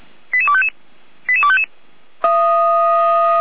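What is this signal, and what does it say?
Five-tone selective call (ZVEI), heard as quick runs of five tones, tones of about 70 ms. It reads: 88255 88255
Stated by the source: ZVEI five-tone call received by SDR, recorded by the Orlamünde fire brigade (Feuerwehr Orlamünde).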